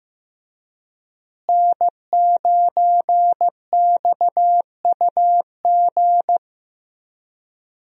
Morse code, heard as a keyed tone, about 15 wpm, 700 Hz. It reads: N9XUG
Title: USA Call Signs - 5 Characters - Repeated 3x - 15wpm